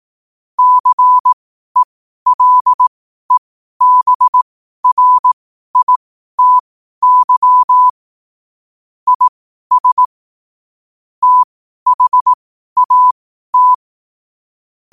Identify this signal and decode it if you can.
Morse code, heard as a keyed tone, about 18 wpm, spaced as for 12 wpm, 1000 Hz. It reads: CELEBRITY IS THAT
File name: qotdfiles / 12wpm